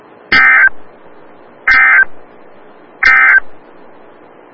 Emergency Alert System (SAME) tones, heard as harsh buzzing bursts. It is heard from 0.2 to 3.5 s.
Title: Des Moines, IA NOAA Weather Radio - WXL57